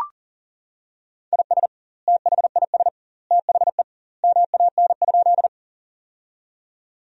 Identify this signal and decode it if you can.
Morse code, heard as a keyed tone, about 40 wpm, 700 Hz. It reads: IS THIS THE MAN?
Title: 2x Word-Spacing - Sentences from Top 200 Words - 40wpm